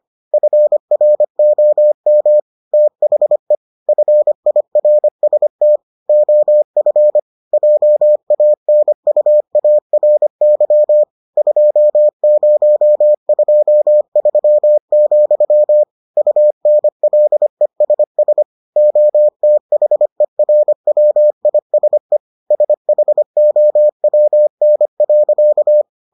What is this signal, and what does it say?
Morse code, heard as a keyed tone, about 25 wpm, 600 Hz. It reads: FROM THE FIRST OF JANUARY 2023, UNLESS OTHERWISE SHOWN.